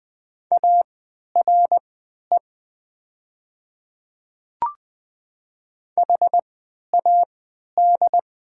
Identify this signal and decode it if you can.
Morse code, heard as a keyed tone, about 20 wpm, spaced as for 11 wpm, 700 Hz. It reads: ARE HAD
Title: 3x Character Spacing - Top 100 Words - 20wpm